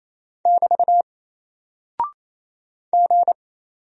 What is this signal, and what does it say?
Morse code, heard as a keyed tone, about 27 wpm, 700 Hz. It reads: <BT> G
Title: Single Character (Letter:Number:Punctuation:Prosign) Alt 28wpm